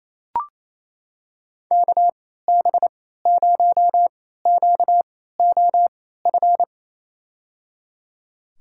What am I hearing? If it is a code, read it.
Morse code, heard as a keyed tone, about 28 wpm, spaced as for 16 wpm, 700 Hz. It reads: KB0QOF
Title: USA Call Signs - 6 Characters - 3x Character Spacing - 28wpm